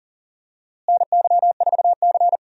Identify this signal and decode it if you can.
Morse code, heard as a keyed tone, about 40 wpm, 700 Hz. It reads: NY4C
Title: USA Call Signs - 4 Characters - Repeated 3x - 40wpm